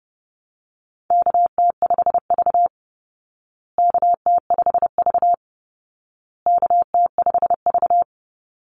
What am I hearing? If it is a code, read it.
Morse code, heard as a keyed tone, about 30 wpm, 700 Hz. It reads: KT5V KT5V KT5V
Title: Call Signs Repeated 2x - 30wpm